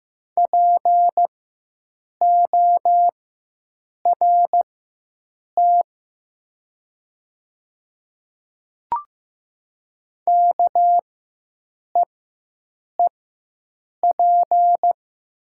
Morse code, heard as a keyed tone, about 15 wpm, spaced as for 7 wpm, 700 Hz. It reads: PORT KEEP
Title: 4x Character Spacing - Top 300 Words - 15wpm